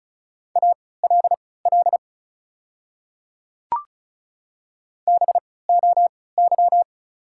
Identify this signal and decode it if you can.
Morse code, heard as a keyed tone, about 36 wpm, spaced as for 20 wpm, 690 Hz. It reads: ALL BOY
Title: 3x Character Spacing - Top 300 Words - 35wpm